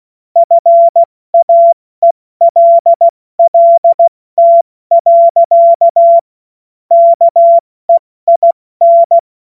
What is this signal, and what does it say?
Morse code, heard as a keyed tone, about 16 wpm, 680 Hz. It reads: FAELLT. KEIN
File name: Morsekurs / Witze / Witz 07. Dezember 2021, 70 F80